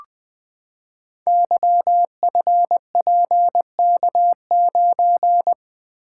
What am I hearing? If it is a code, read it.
Morse code, heard as a keyed tone, about 20 wpm, 700 Hz. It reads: YFPK9
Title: Code Groups - 5 Letter:Numbers - 20wpm